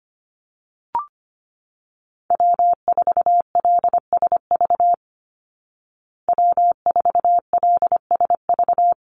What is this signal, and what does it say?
Morse code, heard as a keyed tone, about 25 wpm, 700 Hz. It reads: W4LSV W4LSV
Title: Call Signs Repeated 3x - 25wpm